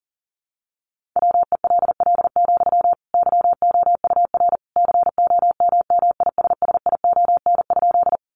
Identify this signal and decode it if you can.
Morse code, heard as a keyed tone, about 40 wpm, 700 Hz. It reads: WELL, YOUR COMMISSION?
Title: Sentences from Top 1400 Words 40wpm